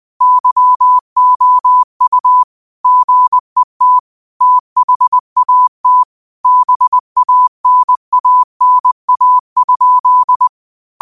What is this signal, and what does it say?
Morse code, heard as a keyed tone, about 20 wpm, 1000 Hz. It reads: YOU GET THAT BANANA?
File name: qotdfiles / 20wpm